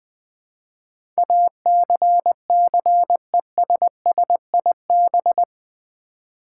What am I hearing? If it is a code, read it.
Morse code, heard as a keyed tone, about 21 wpm, 700 Hz. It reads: ACCESSIB
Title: Build A Word - 4 Letters At A Time - 10-Letter Words - 20wpm